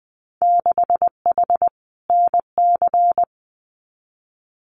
Morse code, heard as a keyed tone, about 20 wpm, 700 Hz. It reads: 6H NC
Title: Winter Field Day Exchanges Encore 20wpm